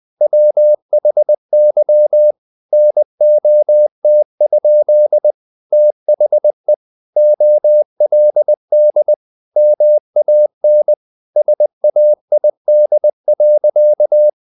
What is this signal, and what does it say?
Morse code, heard as a keyed tone, about 20 wpm, 600 Hz. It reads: WHY NOT? THE OLD MAN SAID.
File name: the old man and the sea / the old man and the sea part 1 20wpm